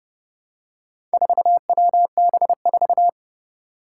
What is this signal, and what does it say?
Morse code, heard as a keyed tone, about 31 wpm, 700 Hz. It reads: 4WB4
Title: Four Challenging Letter-Number Combinations 30wpm